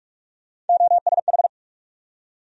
Morse code, heard as a keyed tone, about 47 wpm, 700 Hz. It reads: OSH
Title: USA Call Sign Postfixes - 3 Letters 45wpm